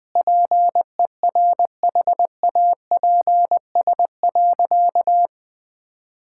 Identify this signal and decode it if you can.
Morse code, heard as a keyed tone, about 20 wpm, 700 Hz. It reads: PERHAPS.
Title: Sentences from Top 1600 Words 20wpm